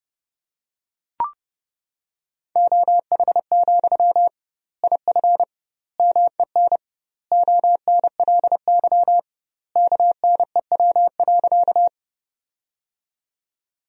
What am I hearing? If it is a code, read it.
Morse code, heard as a keyed tone, about 30 wpm, 700 Hz. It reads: OH, IF MEN ONLY KNEW.
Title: 2x Word-Spacing - Sentences from Top 500 Words - 30wpm